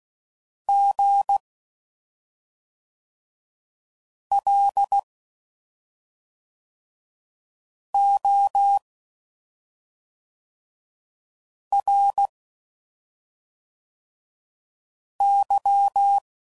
Morse code, heard as a keyed tone, about 16 wpm, 800 Hz. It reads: G L O R Y
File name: Suffixes / Special ING's / SlowToFaster ING's